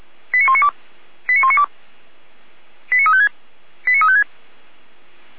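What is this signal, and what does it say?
Five-tone selective call (ZVEI), heard as quick runs of five tones, tones of about 80 ms. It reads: 89192 89192 89356 89356
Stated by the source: ZVEI five-tone call 89391 received by SDR on channel 455, fire brigade Orlamuende (feuerwehr-orlamuende).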